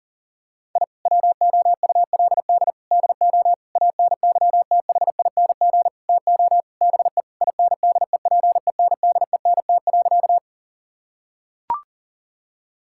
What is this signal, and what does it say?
Morse code, heard as a keyed tone, about 40 wpm, 700 Hz. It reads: I WOULD DO ANYTHING TO BE INDEPENDENT.
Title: Sentences from Top 1000 Words - Review Part 6 - 40wpm